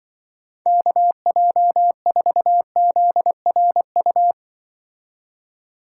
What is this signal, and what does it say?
Morse code, heard as a keyed tone, about 24 wpm, 700 Hz. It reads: KJ4ZRU
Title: Speed Racing - USA 2x3 Call Signs 20wpm